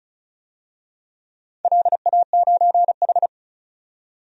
Morse code, heard as a keyed tone, about 36 wpm, 700 Hz. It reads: LA9H
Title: DXCC-100 Call Signs - 4 Characters - 35wpm